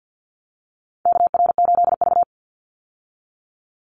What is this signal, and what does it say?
Morse code, heard as a keyed tone, about 50 wpm, 700 Hz. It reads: KR8V
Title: Call Signs - 50wpm